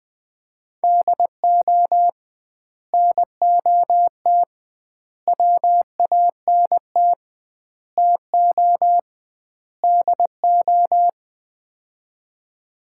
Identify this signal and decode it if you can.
Morse code, heard as a keyed tone, about 21 wpm, 700 Hz. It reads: DO NOT WANT TO DO
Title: 2x Word-Spacing - Sets of 5 Words - 20wpm